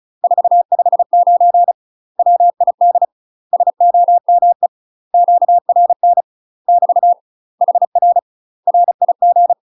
Morse code, heard as a keyed tone, about 36 wpm, 700 Hz. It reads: 459 WID SOME QRN <BT> HR RIG
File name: UK MASTER QSOs TRAINING / 35WPM / UK MASTER QSOs 0002